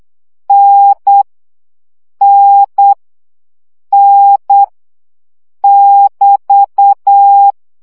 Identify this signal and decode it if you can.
Morse code, heard as a keyed tone, about 8 wpm, 800 Hz. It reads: N N N <BT>
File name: CW gyakorlas n 7 baud